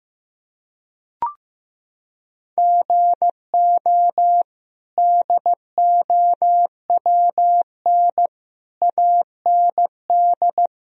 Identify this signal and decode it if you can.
Morse code, heard as a keyed tone, about 15 wpm, 700 Hz. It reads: GO DOWN AND
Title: Sentences from Top 1000 Words - Review Part 1 - 15wpm